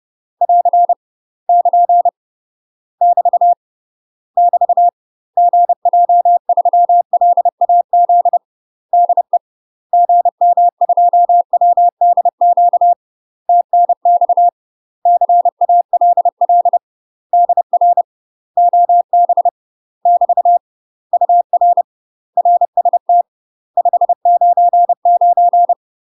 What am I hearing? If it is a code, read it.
Morse code, heard as a keyed tone, about 30 wpm, 700 Hz. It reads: <AR> <KN> <BT> <BT> GJ3LAZ DE GM2WDQ TNX CALL DR OB <BT> UR RST 599
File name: UK MASTER QSOs TRAINING / 30WPM / UK MASTER QSOs 0019